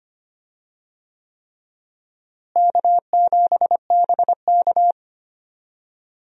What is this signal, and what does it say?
Morse code, heard as a keyed tone, about 25 wpm, 700 Hz. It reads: K7BK
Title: USA Call Signs - 4 Characters - Repeated 3x - 25wpm